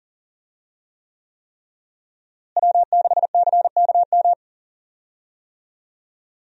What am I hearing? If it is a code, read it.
Morse code, heard as a keyed tone, about 40 wpm, 700 Hz. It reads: W6CKM
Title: USA Call Signs - 5 Characters - Repeated 3x - 40wpm